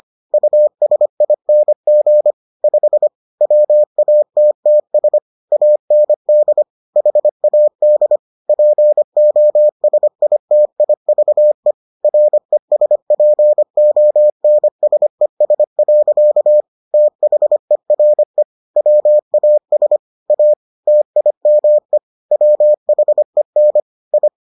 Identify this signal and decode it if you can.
Morse code, heard as a keyed tone, about 25 wpm, 600 Hz. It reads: USING 5 WATTS AND HAD POSITIVE RESPONSES. THERE WAS A TIME WHEN I